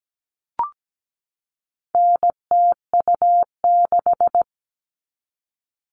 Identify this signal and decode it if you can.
Morse code, heard as a keyed tone, about 17 wpm, 700 Hz. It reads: NTU6
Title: All Cut Numbers - 4-Digit 17wpm